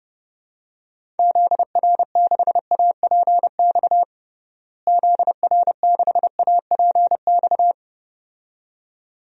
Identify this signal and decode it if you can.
Morse code, heard as a keyed tone, about 30 wpm, 700 Hz. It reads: ZR6APX ZR6APX
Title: DXCC-100 Call Signs - 6 Characters - Repeated 2x - 30wpm